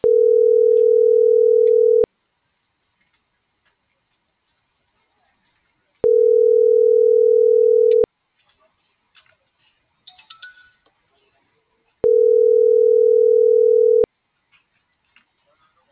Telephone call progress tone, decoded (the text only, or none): ringback tone